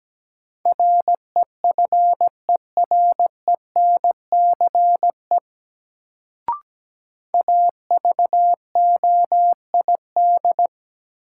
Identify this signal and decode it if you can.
Morse code, heard as a keyed tone, about 17 wpm, 700 Hz. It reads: REFERENCE AVOID
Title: Sentences from Top 1200 Words - New Word Drill Alt - 17wpm